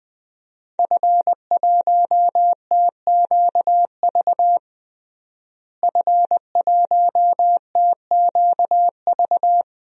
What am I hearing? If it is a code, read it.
Morse code, heard as a keyed tone, about 20 wpm, 700 Hz. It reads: F1TQV F1TQV
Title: DXCC-100 Call Signs - 5 Characters - Repeated 2x - 20wpm